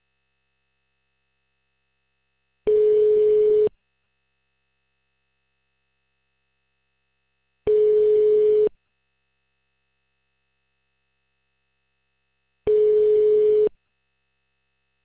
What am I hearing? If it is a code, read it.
Ringback tone (Europe, 425 Hz).